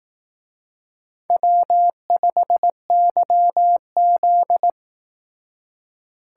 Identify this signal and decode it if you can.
Morse code, heard as a keyed tone, about 18 wpm, 700 Hz. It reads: W5YZ
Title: Speed Racing - USA 1x2 Call Signs 15wpm